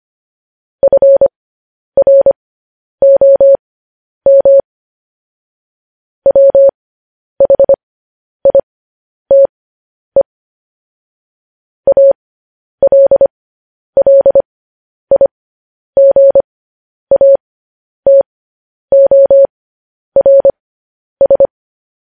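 Morse code, heard as a keyed tone, about 25 wpm, spaced as for 10 wpm, 550 Hz. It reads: FROM WHITE ALLIGATORS
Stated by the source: Morse code sending of news headlines, speed 10.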